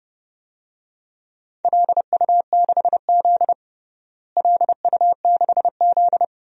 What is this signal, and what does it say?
Morse code, heard as a keyed tone, about 32 wpm, 700 Hz. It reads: LU6Z LU6Z
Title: DXCC-100 Call Signs - 4 Characters - Repeated 2x - 30wpm